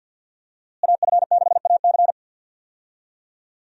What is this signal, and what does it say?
Morse code, heard as a keyed tone, about 52 wpm, 700 Hz. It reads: AF6AC
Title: USA Call Signs - 5 Characters - Repeated 3x - 50wpm